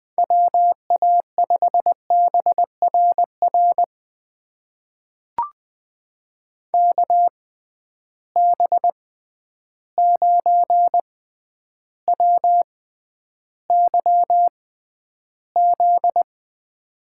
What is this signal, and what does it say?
Morse code, heard as a keyed tone, about 20 wpm, spaced as for 10 wpm, 700 Hz. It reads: WA5BRR KB9WYZ
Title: USA Call Signs - 6 Characters - 6x Character Spacing - 20wpm